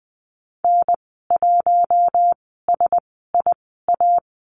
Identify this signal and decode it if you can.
Morse code, heard as a keyed tone, about 19 wpm, spaced as for 14 wpm, 700 Hz. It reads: N1SIA
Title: 055 Lesson 10g - Letter r - Call Signs 20wpm